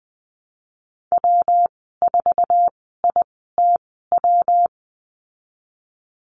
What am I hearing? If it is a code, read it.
Morse code, heard as a keyed tone, about 20 wpm, spaced as for 14 wpm, 700 Hz. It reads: W4ITW
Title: 128 Lesson 21g - Number 6 - Call Signs 20wpm